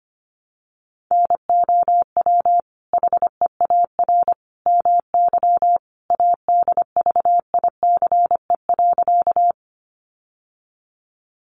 Morse code, heard as a keyed tone, about 25 wpm, 700 Hz. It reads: NOW HEAR MY ADVICE.
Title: Sentences from Top 1100 Words 25wpm